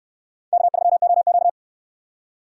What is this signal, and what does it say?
Morse code, heard as a keyed tone, about 64 wpm, 700 Hz. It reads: K3OQ